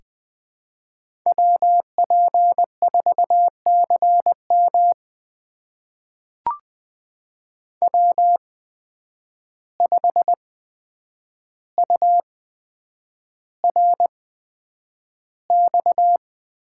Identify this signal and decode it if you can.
Morse code, heard as a keyed tone, about 20 wpm, spaced as for 9 wpm, 700 Hz. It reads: WP4CM W5URX
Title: USA Call Signs - 8x Character Spacing - 20wpm